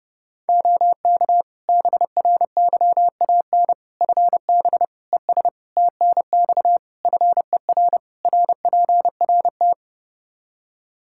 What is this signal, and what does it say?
Morse code, heard as a keyed tone, about 30 wpm, 700 Hz. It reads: OK BRYAN FB ES TNX FER RPRT
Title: Practice QSO - Exchange 2 - Info - 30wpm